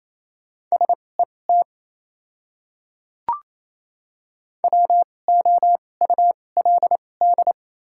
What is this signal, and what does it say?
Morse code, heard as a keyed tone, about 28 wpm, spaced as for 20 wpm, 700 Hz. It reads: SET WOULD
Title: 2x Character Spacing - Top 300 Words - 28wpm